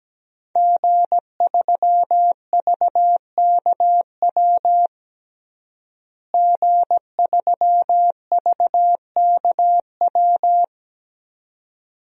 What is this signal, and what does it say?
Morse code, heard as a keyed tone, about 17 wpm, 700 Hz. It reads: G3VKW G3VKW
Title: DXCC-100 Call Signs - 5 Characters - Repeated 2x - 17wpm